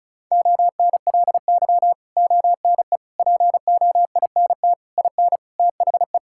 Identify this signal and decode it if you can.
Morse code, heard as a keyed tone, about 34 wpm, 700 Hz. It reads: ONLY ONE POINT IN THE
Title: Sentences from Top 1000 Words - Review Part 5 - 35wpm